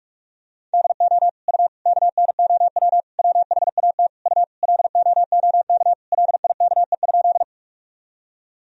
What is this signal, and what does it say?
Morse code, heard as a keyed tone, about 47 wpm, 700 Hz. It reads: DO U KNOW WHAT U LOOK LIKE?
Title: Sentences from Top 100 Words and Abbreviations 45wpm